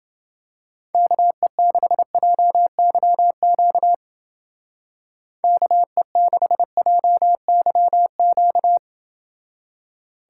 Morse code, heard as a keyed tone, about 29 wpm, 700 Hz. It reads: KE6JYQ KE6JYQ